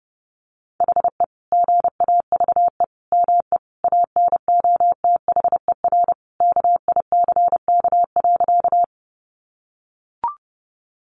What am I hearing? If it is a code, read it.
Morse code, heard as a keyed tone, about 30 wpm, 700 Hz. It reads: HE GAVE ME ANOTHER KICK.
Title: Sentences from Top 1300 Words 30wpm